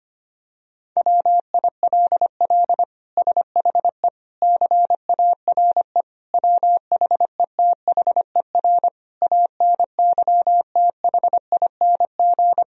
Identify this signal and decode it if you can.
Morse code, heard as a keyed tone, about 25 wpm, 700 Hz. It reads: WILL SHE CARE WHETHER ANYTHING
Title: Sentences from Top 1000 Words - Review Part 3 - 25wpm